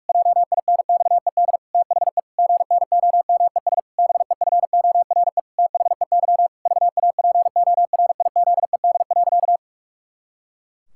Morse code, heard as a keyed tone, about 41 wpm, 700 Hz. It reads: JINXED THE GNOMES BEFORE THEY VAPORIZED.
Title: Pangram Challenge v2 - 45wpm